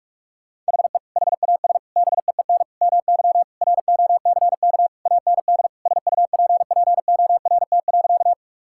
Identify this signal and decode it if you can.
Morse code, heard as a keyed tone, about 46 wpm, 700 Hz. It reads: HE HAS BEEN MY ROCK AND SUPPORT.
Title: Sentences from Top 1000 Words - Review Part 5 - 45wpm